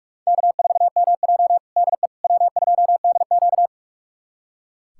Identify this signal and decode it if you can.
Morse code, heard as a keyed tone, about 44 wpm, 700 Hz. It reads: K4MJ DE W2DQ
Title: USA Callsign DE Callsign - 4 Characters - 45wpm